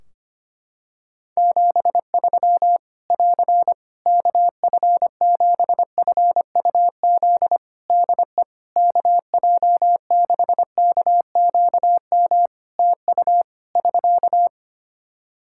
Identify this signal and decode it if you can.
Morse code, heard as a keyed tone, about 25 wpm, 700 Hz. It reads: 73 <AR> KF7FUZ DE KJ6KQM TU <SK>
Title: Practice QSO - Exchange 4 - Ending - 25wpm